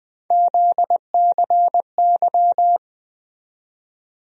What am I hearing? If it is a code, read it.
Morse code, heard as a keyed tone, about 20 wpm, 700 Hz. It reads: ZCY
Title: USA Call Sign Postfixes - 3 Letters 20wpm